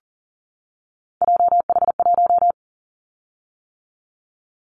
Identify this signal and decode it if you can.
Morse code, heard as a keyed tone, about 40 wpm, 700 Hz. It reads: JH1
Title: Callsign Prefixes for Top 25 DXCC - 40wpm